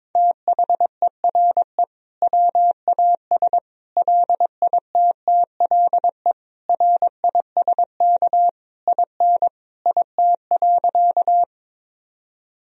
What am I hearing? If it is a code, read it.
Morse code, heard as a keyed tone, about 22 wpm, 700 Hz. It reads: THERE WAS LITTLE RISK IN IT.
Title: Sentences from Top 1000 Words - Review Part 4 - 22wpm